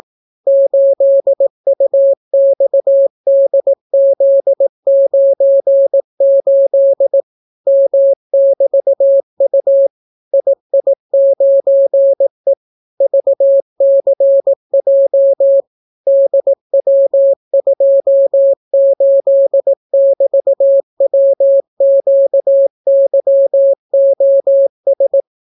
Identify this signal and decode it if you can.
Morse code, heard as a keyed tone, about 18 wpm, 550 Hz. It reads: 8UXDZ98 M<BT>U II9E VCJ DW28<BT>WQYOS